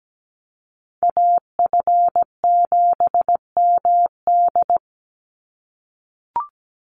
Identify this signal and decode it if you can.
Morse code, heard as a keyed tone, about 17 wpm, 700 Hz. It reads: AF7MD
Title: NA SOTA Chaser Callsigns 17wpm